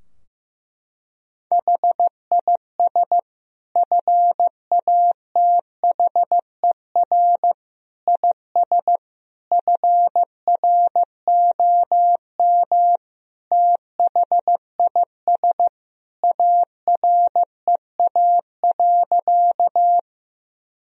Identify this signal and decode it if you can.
Morse code, heard as a keyed tone, about 15 wpm, 700 Hz. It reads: HIS FATHER IS FROM THIS AREA.
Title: Sentences from Top 1000 Words - Review Part 2 - 15wpm